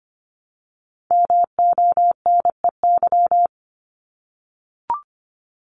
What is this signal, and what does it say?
Morse code, heard as a keyed tone, about 25 wpm, 700 Hz. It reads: MONEY
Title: Multi-letter Phonemes Repeated 3x 25wpm